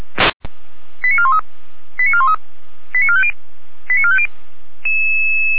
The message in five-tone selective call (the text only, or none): 89413 89413 89450 89450